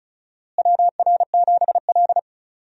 Morse code, heard as a keyed tone, about 34 wpm, 700 Hz. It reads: WR7L
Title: USA Call Signs - 4 Characters - Repeated 3x - 35wpm